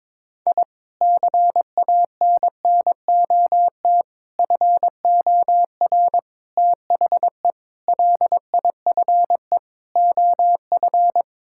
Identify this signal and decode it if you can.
Morse code, heard as a keyed tone, about 22 wpm, 700 Hz. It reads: I CANNOT FOR THE LIFE OF
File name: practice / 22wpm / Sentences with 8-Word Phrases - 22wpm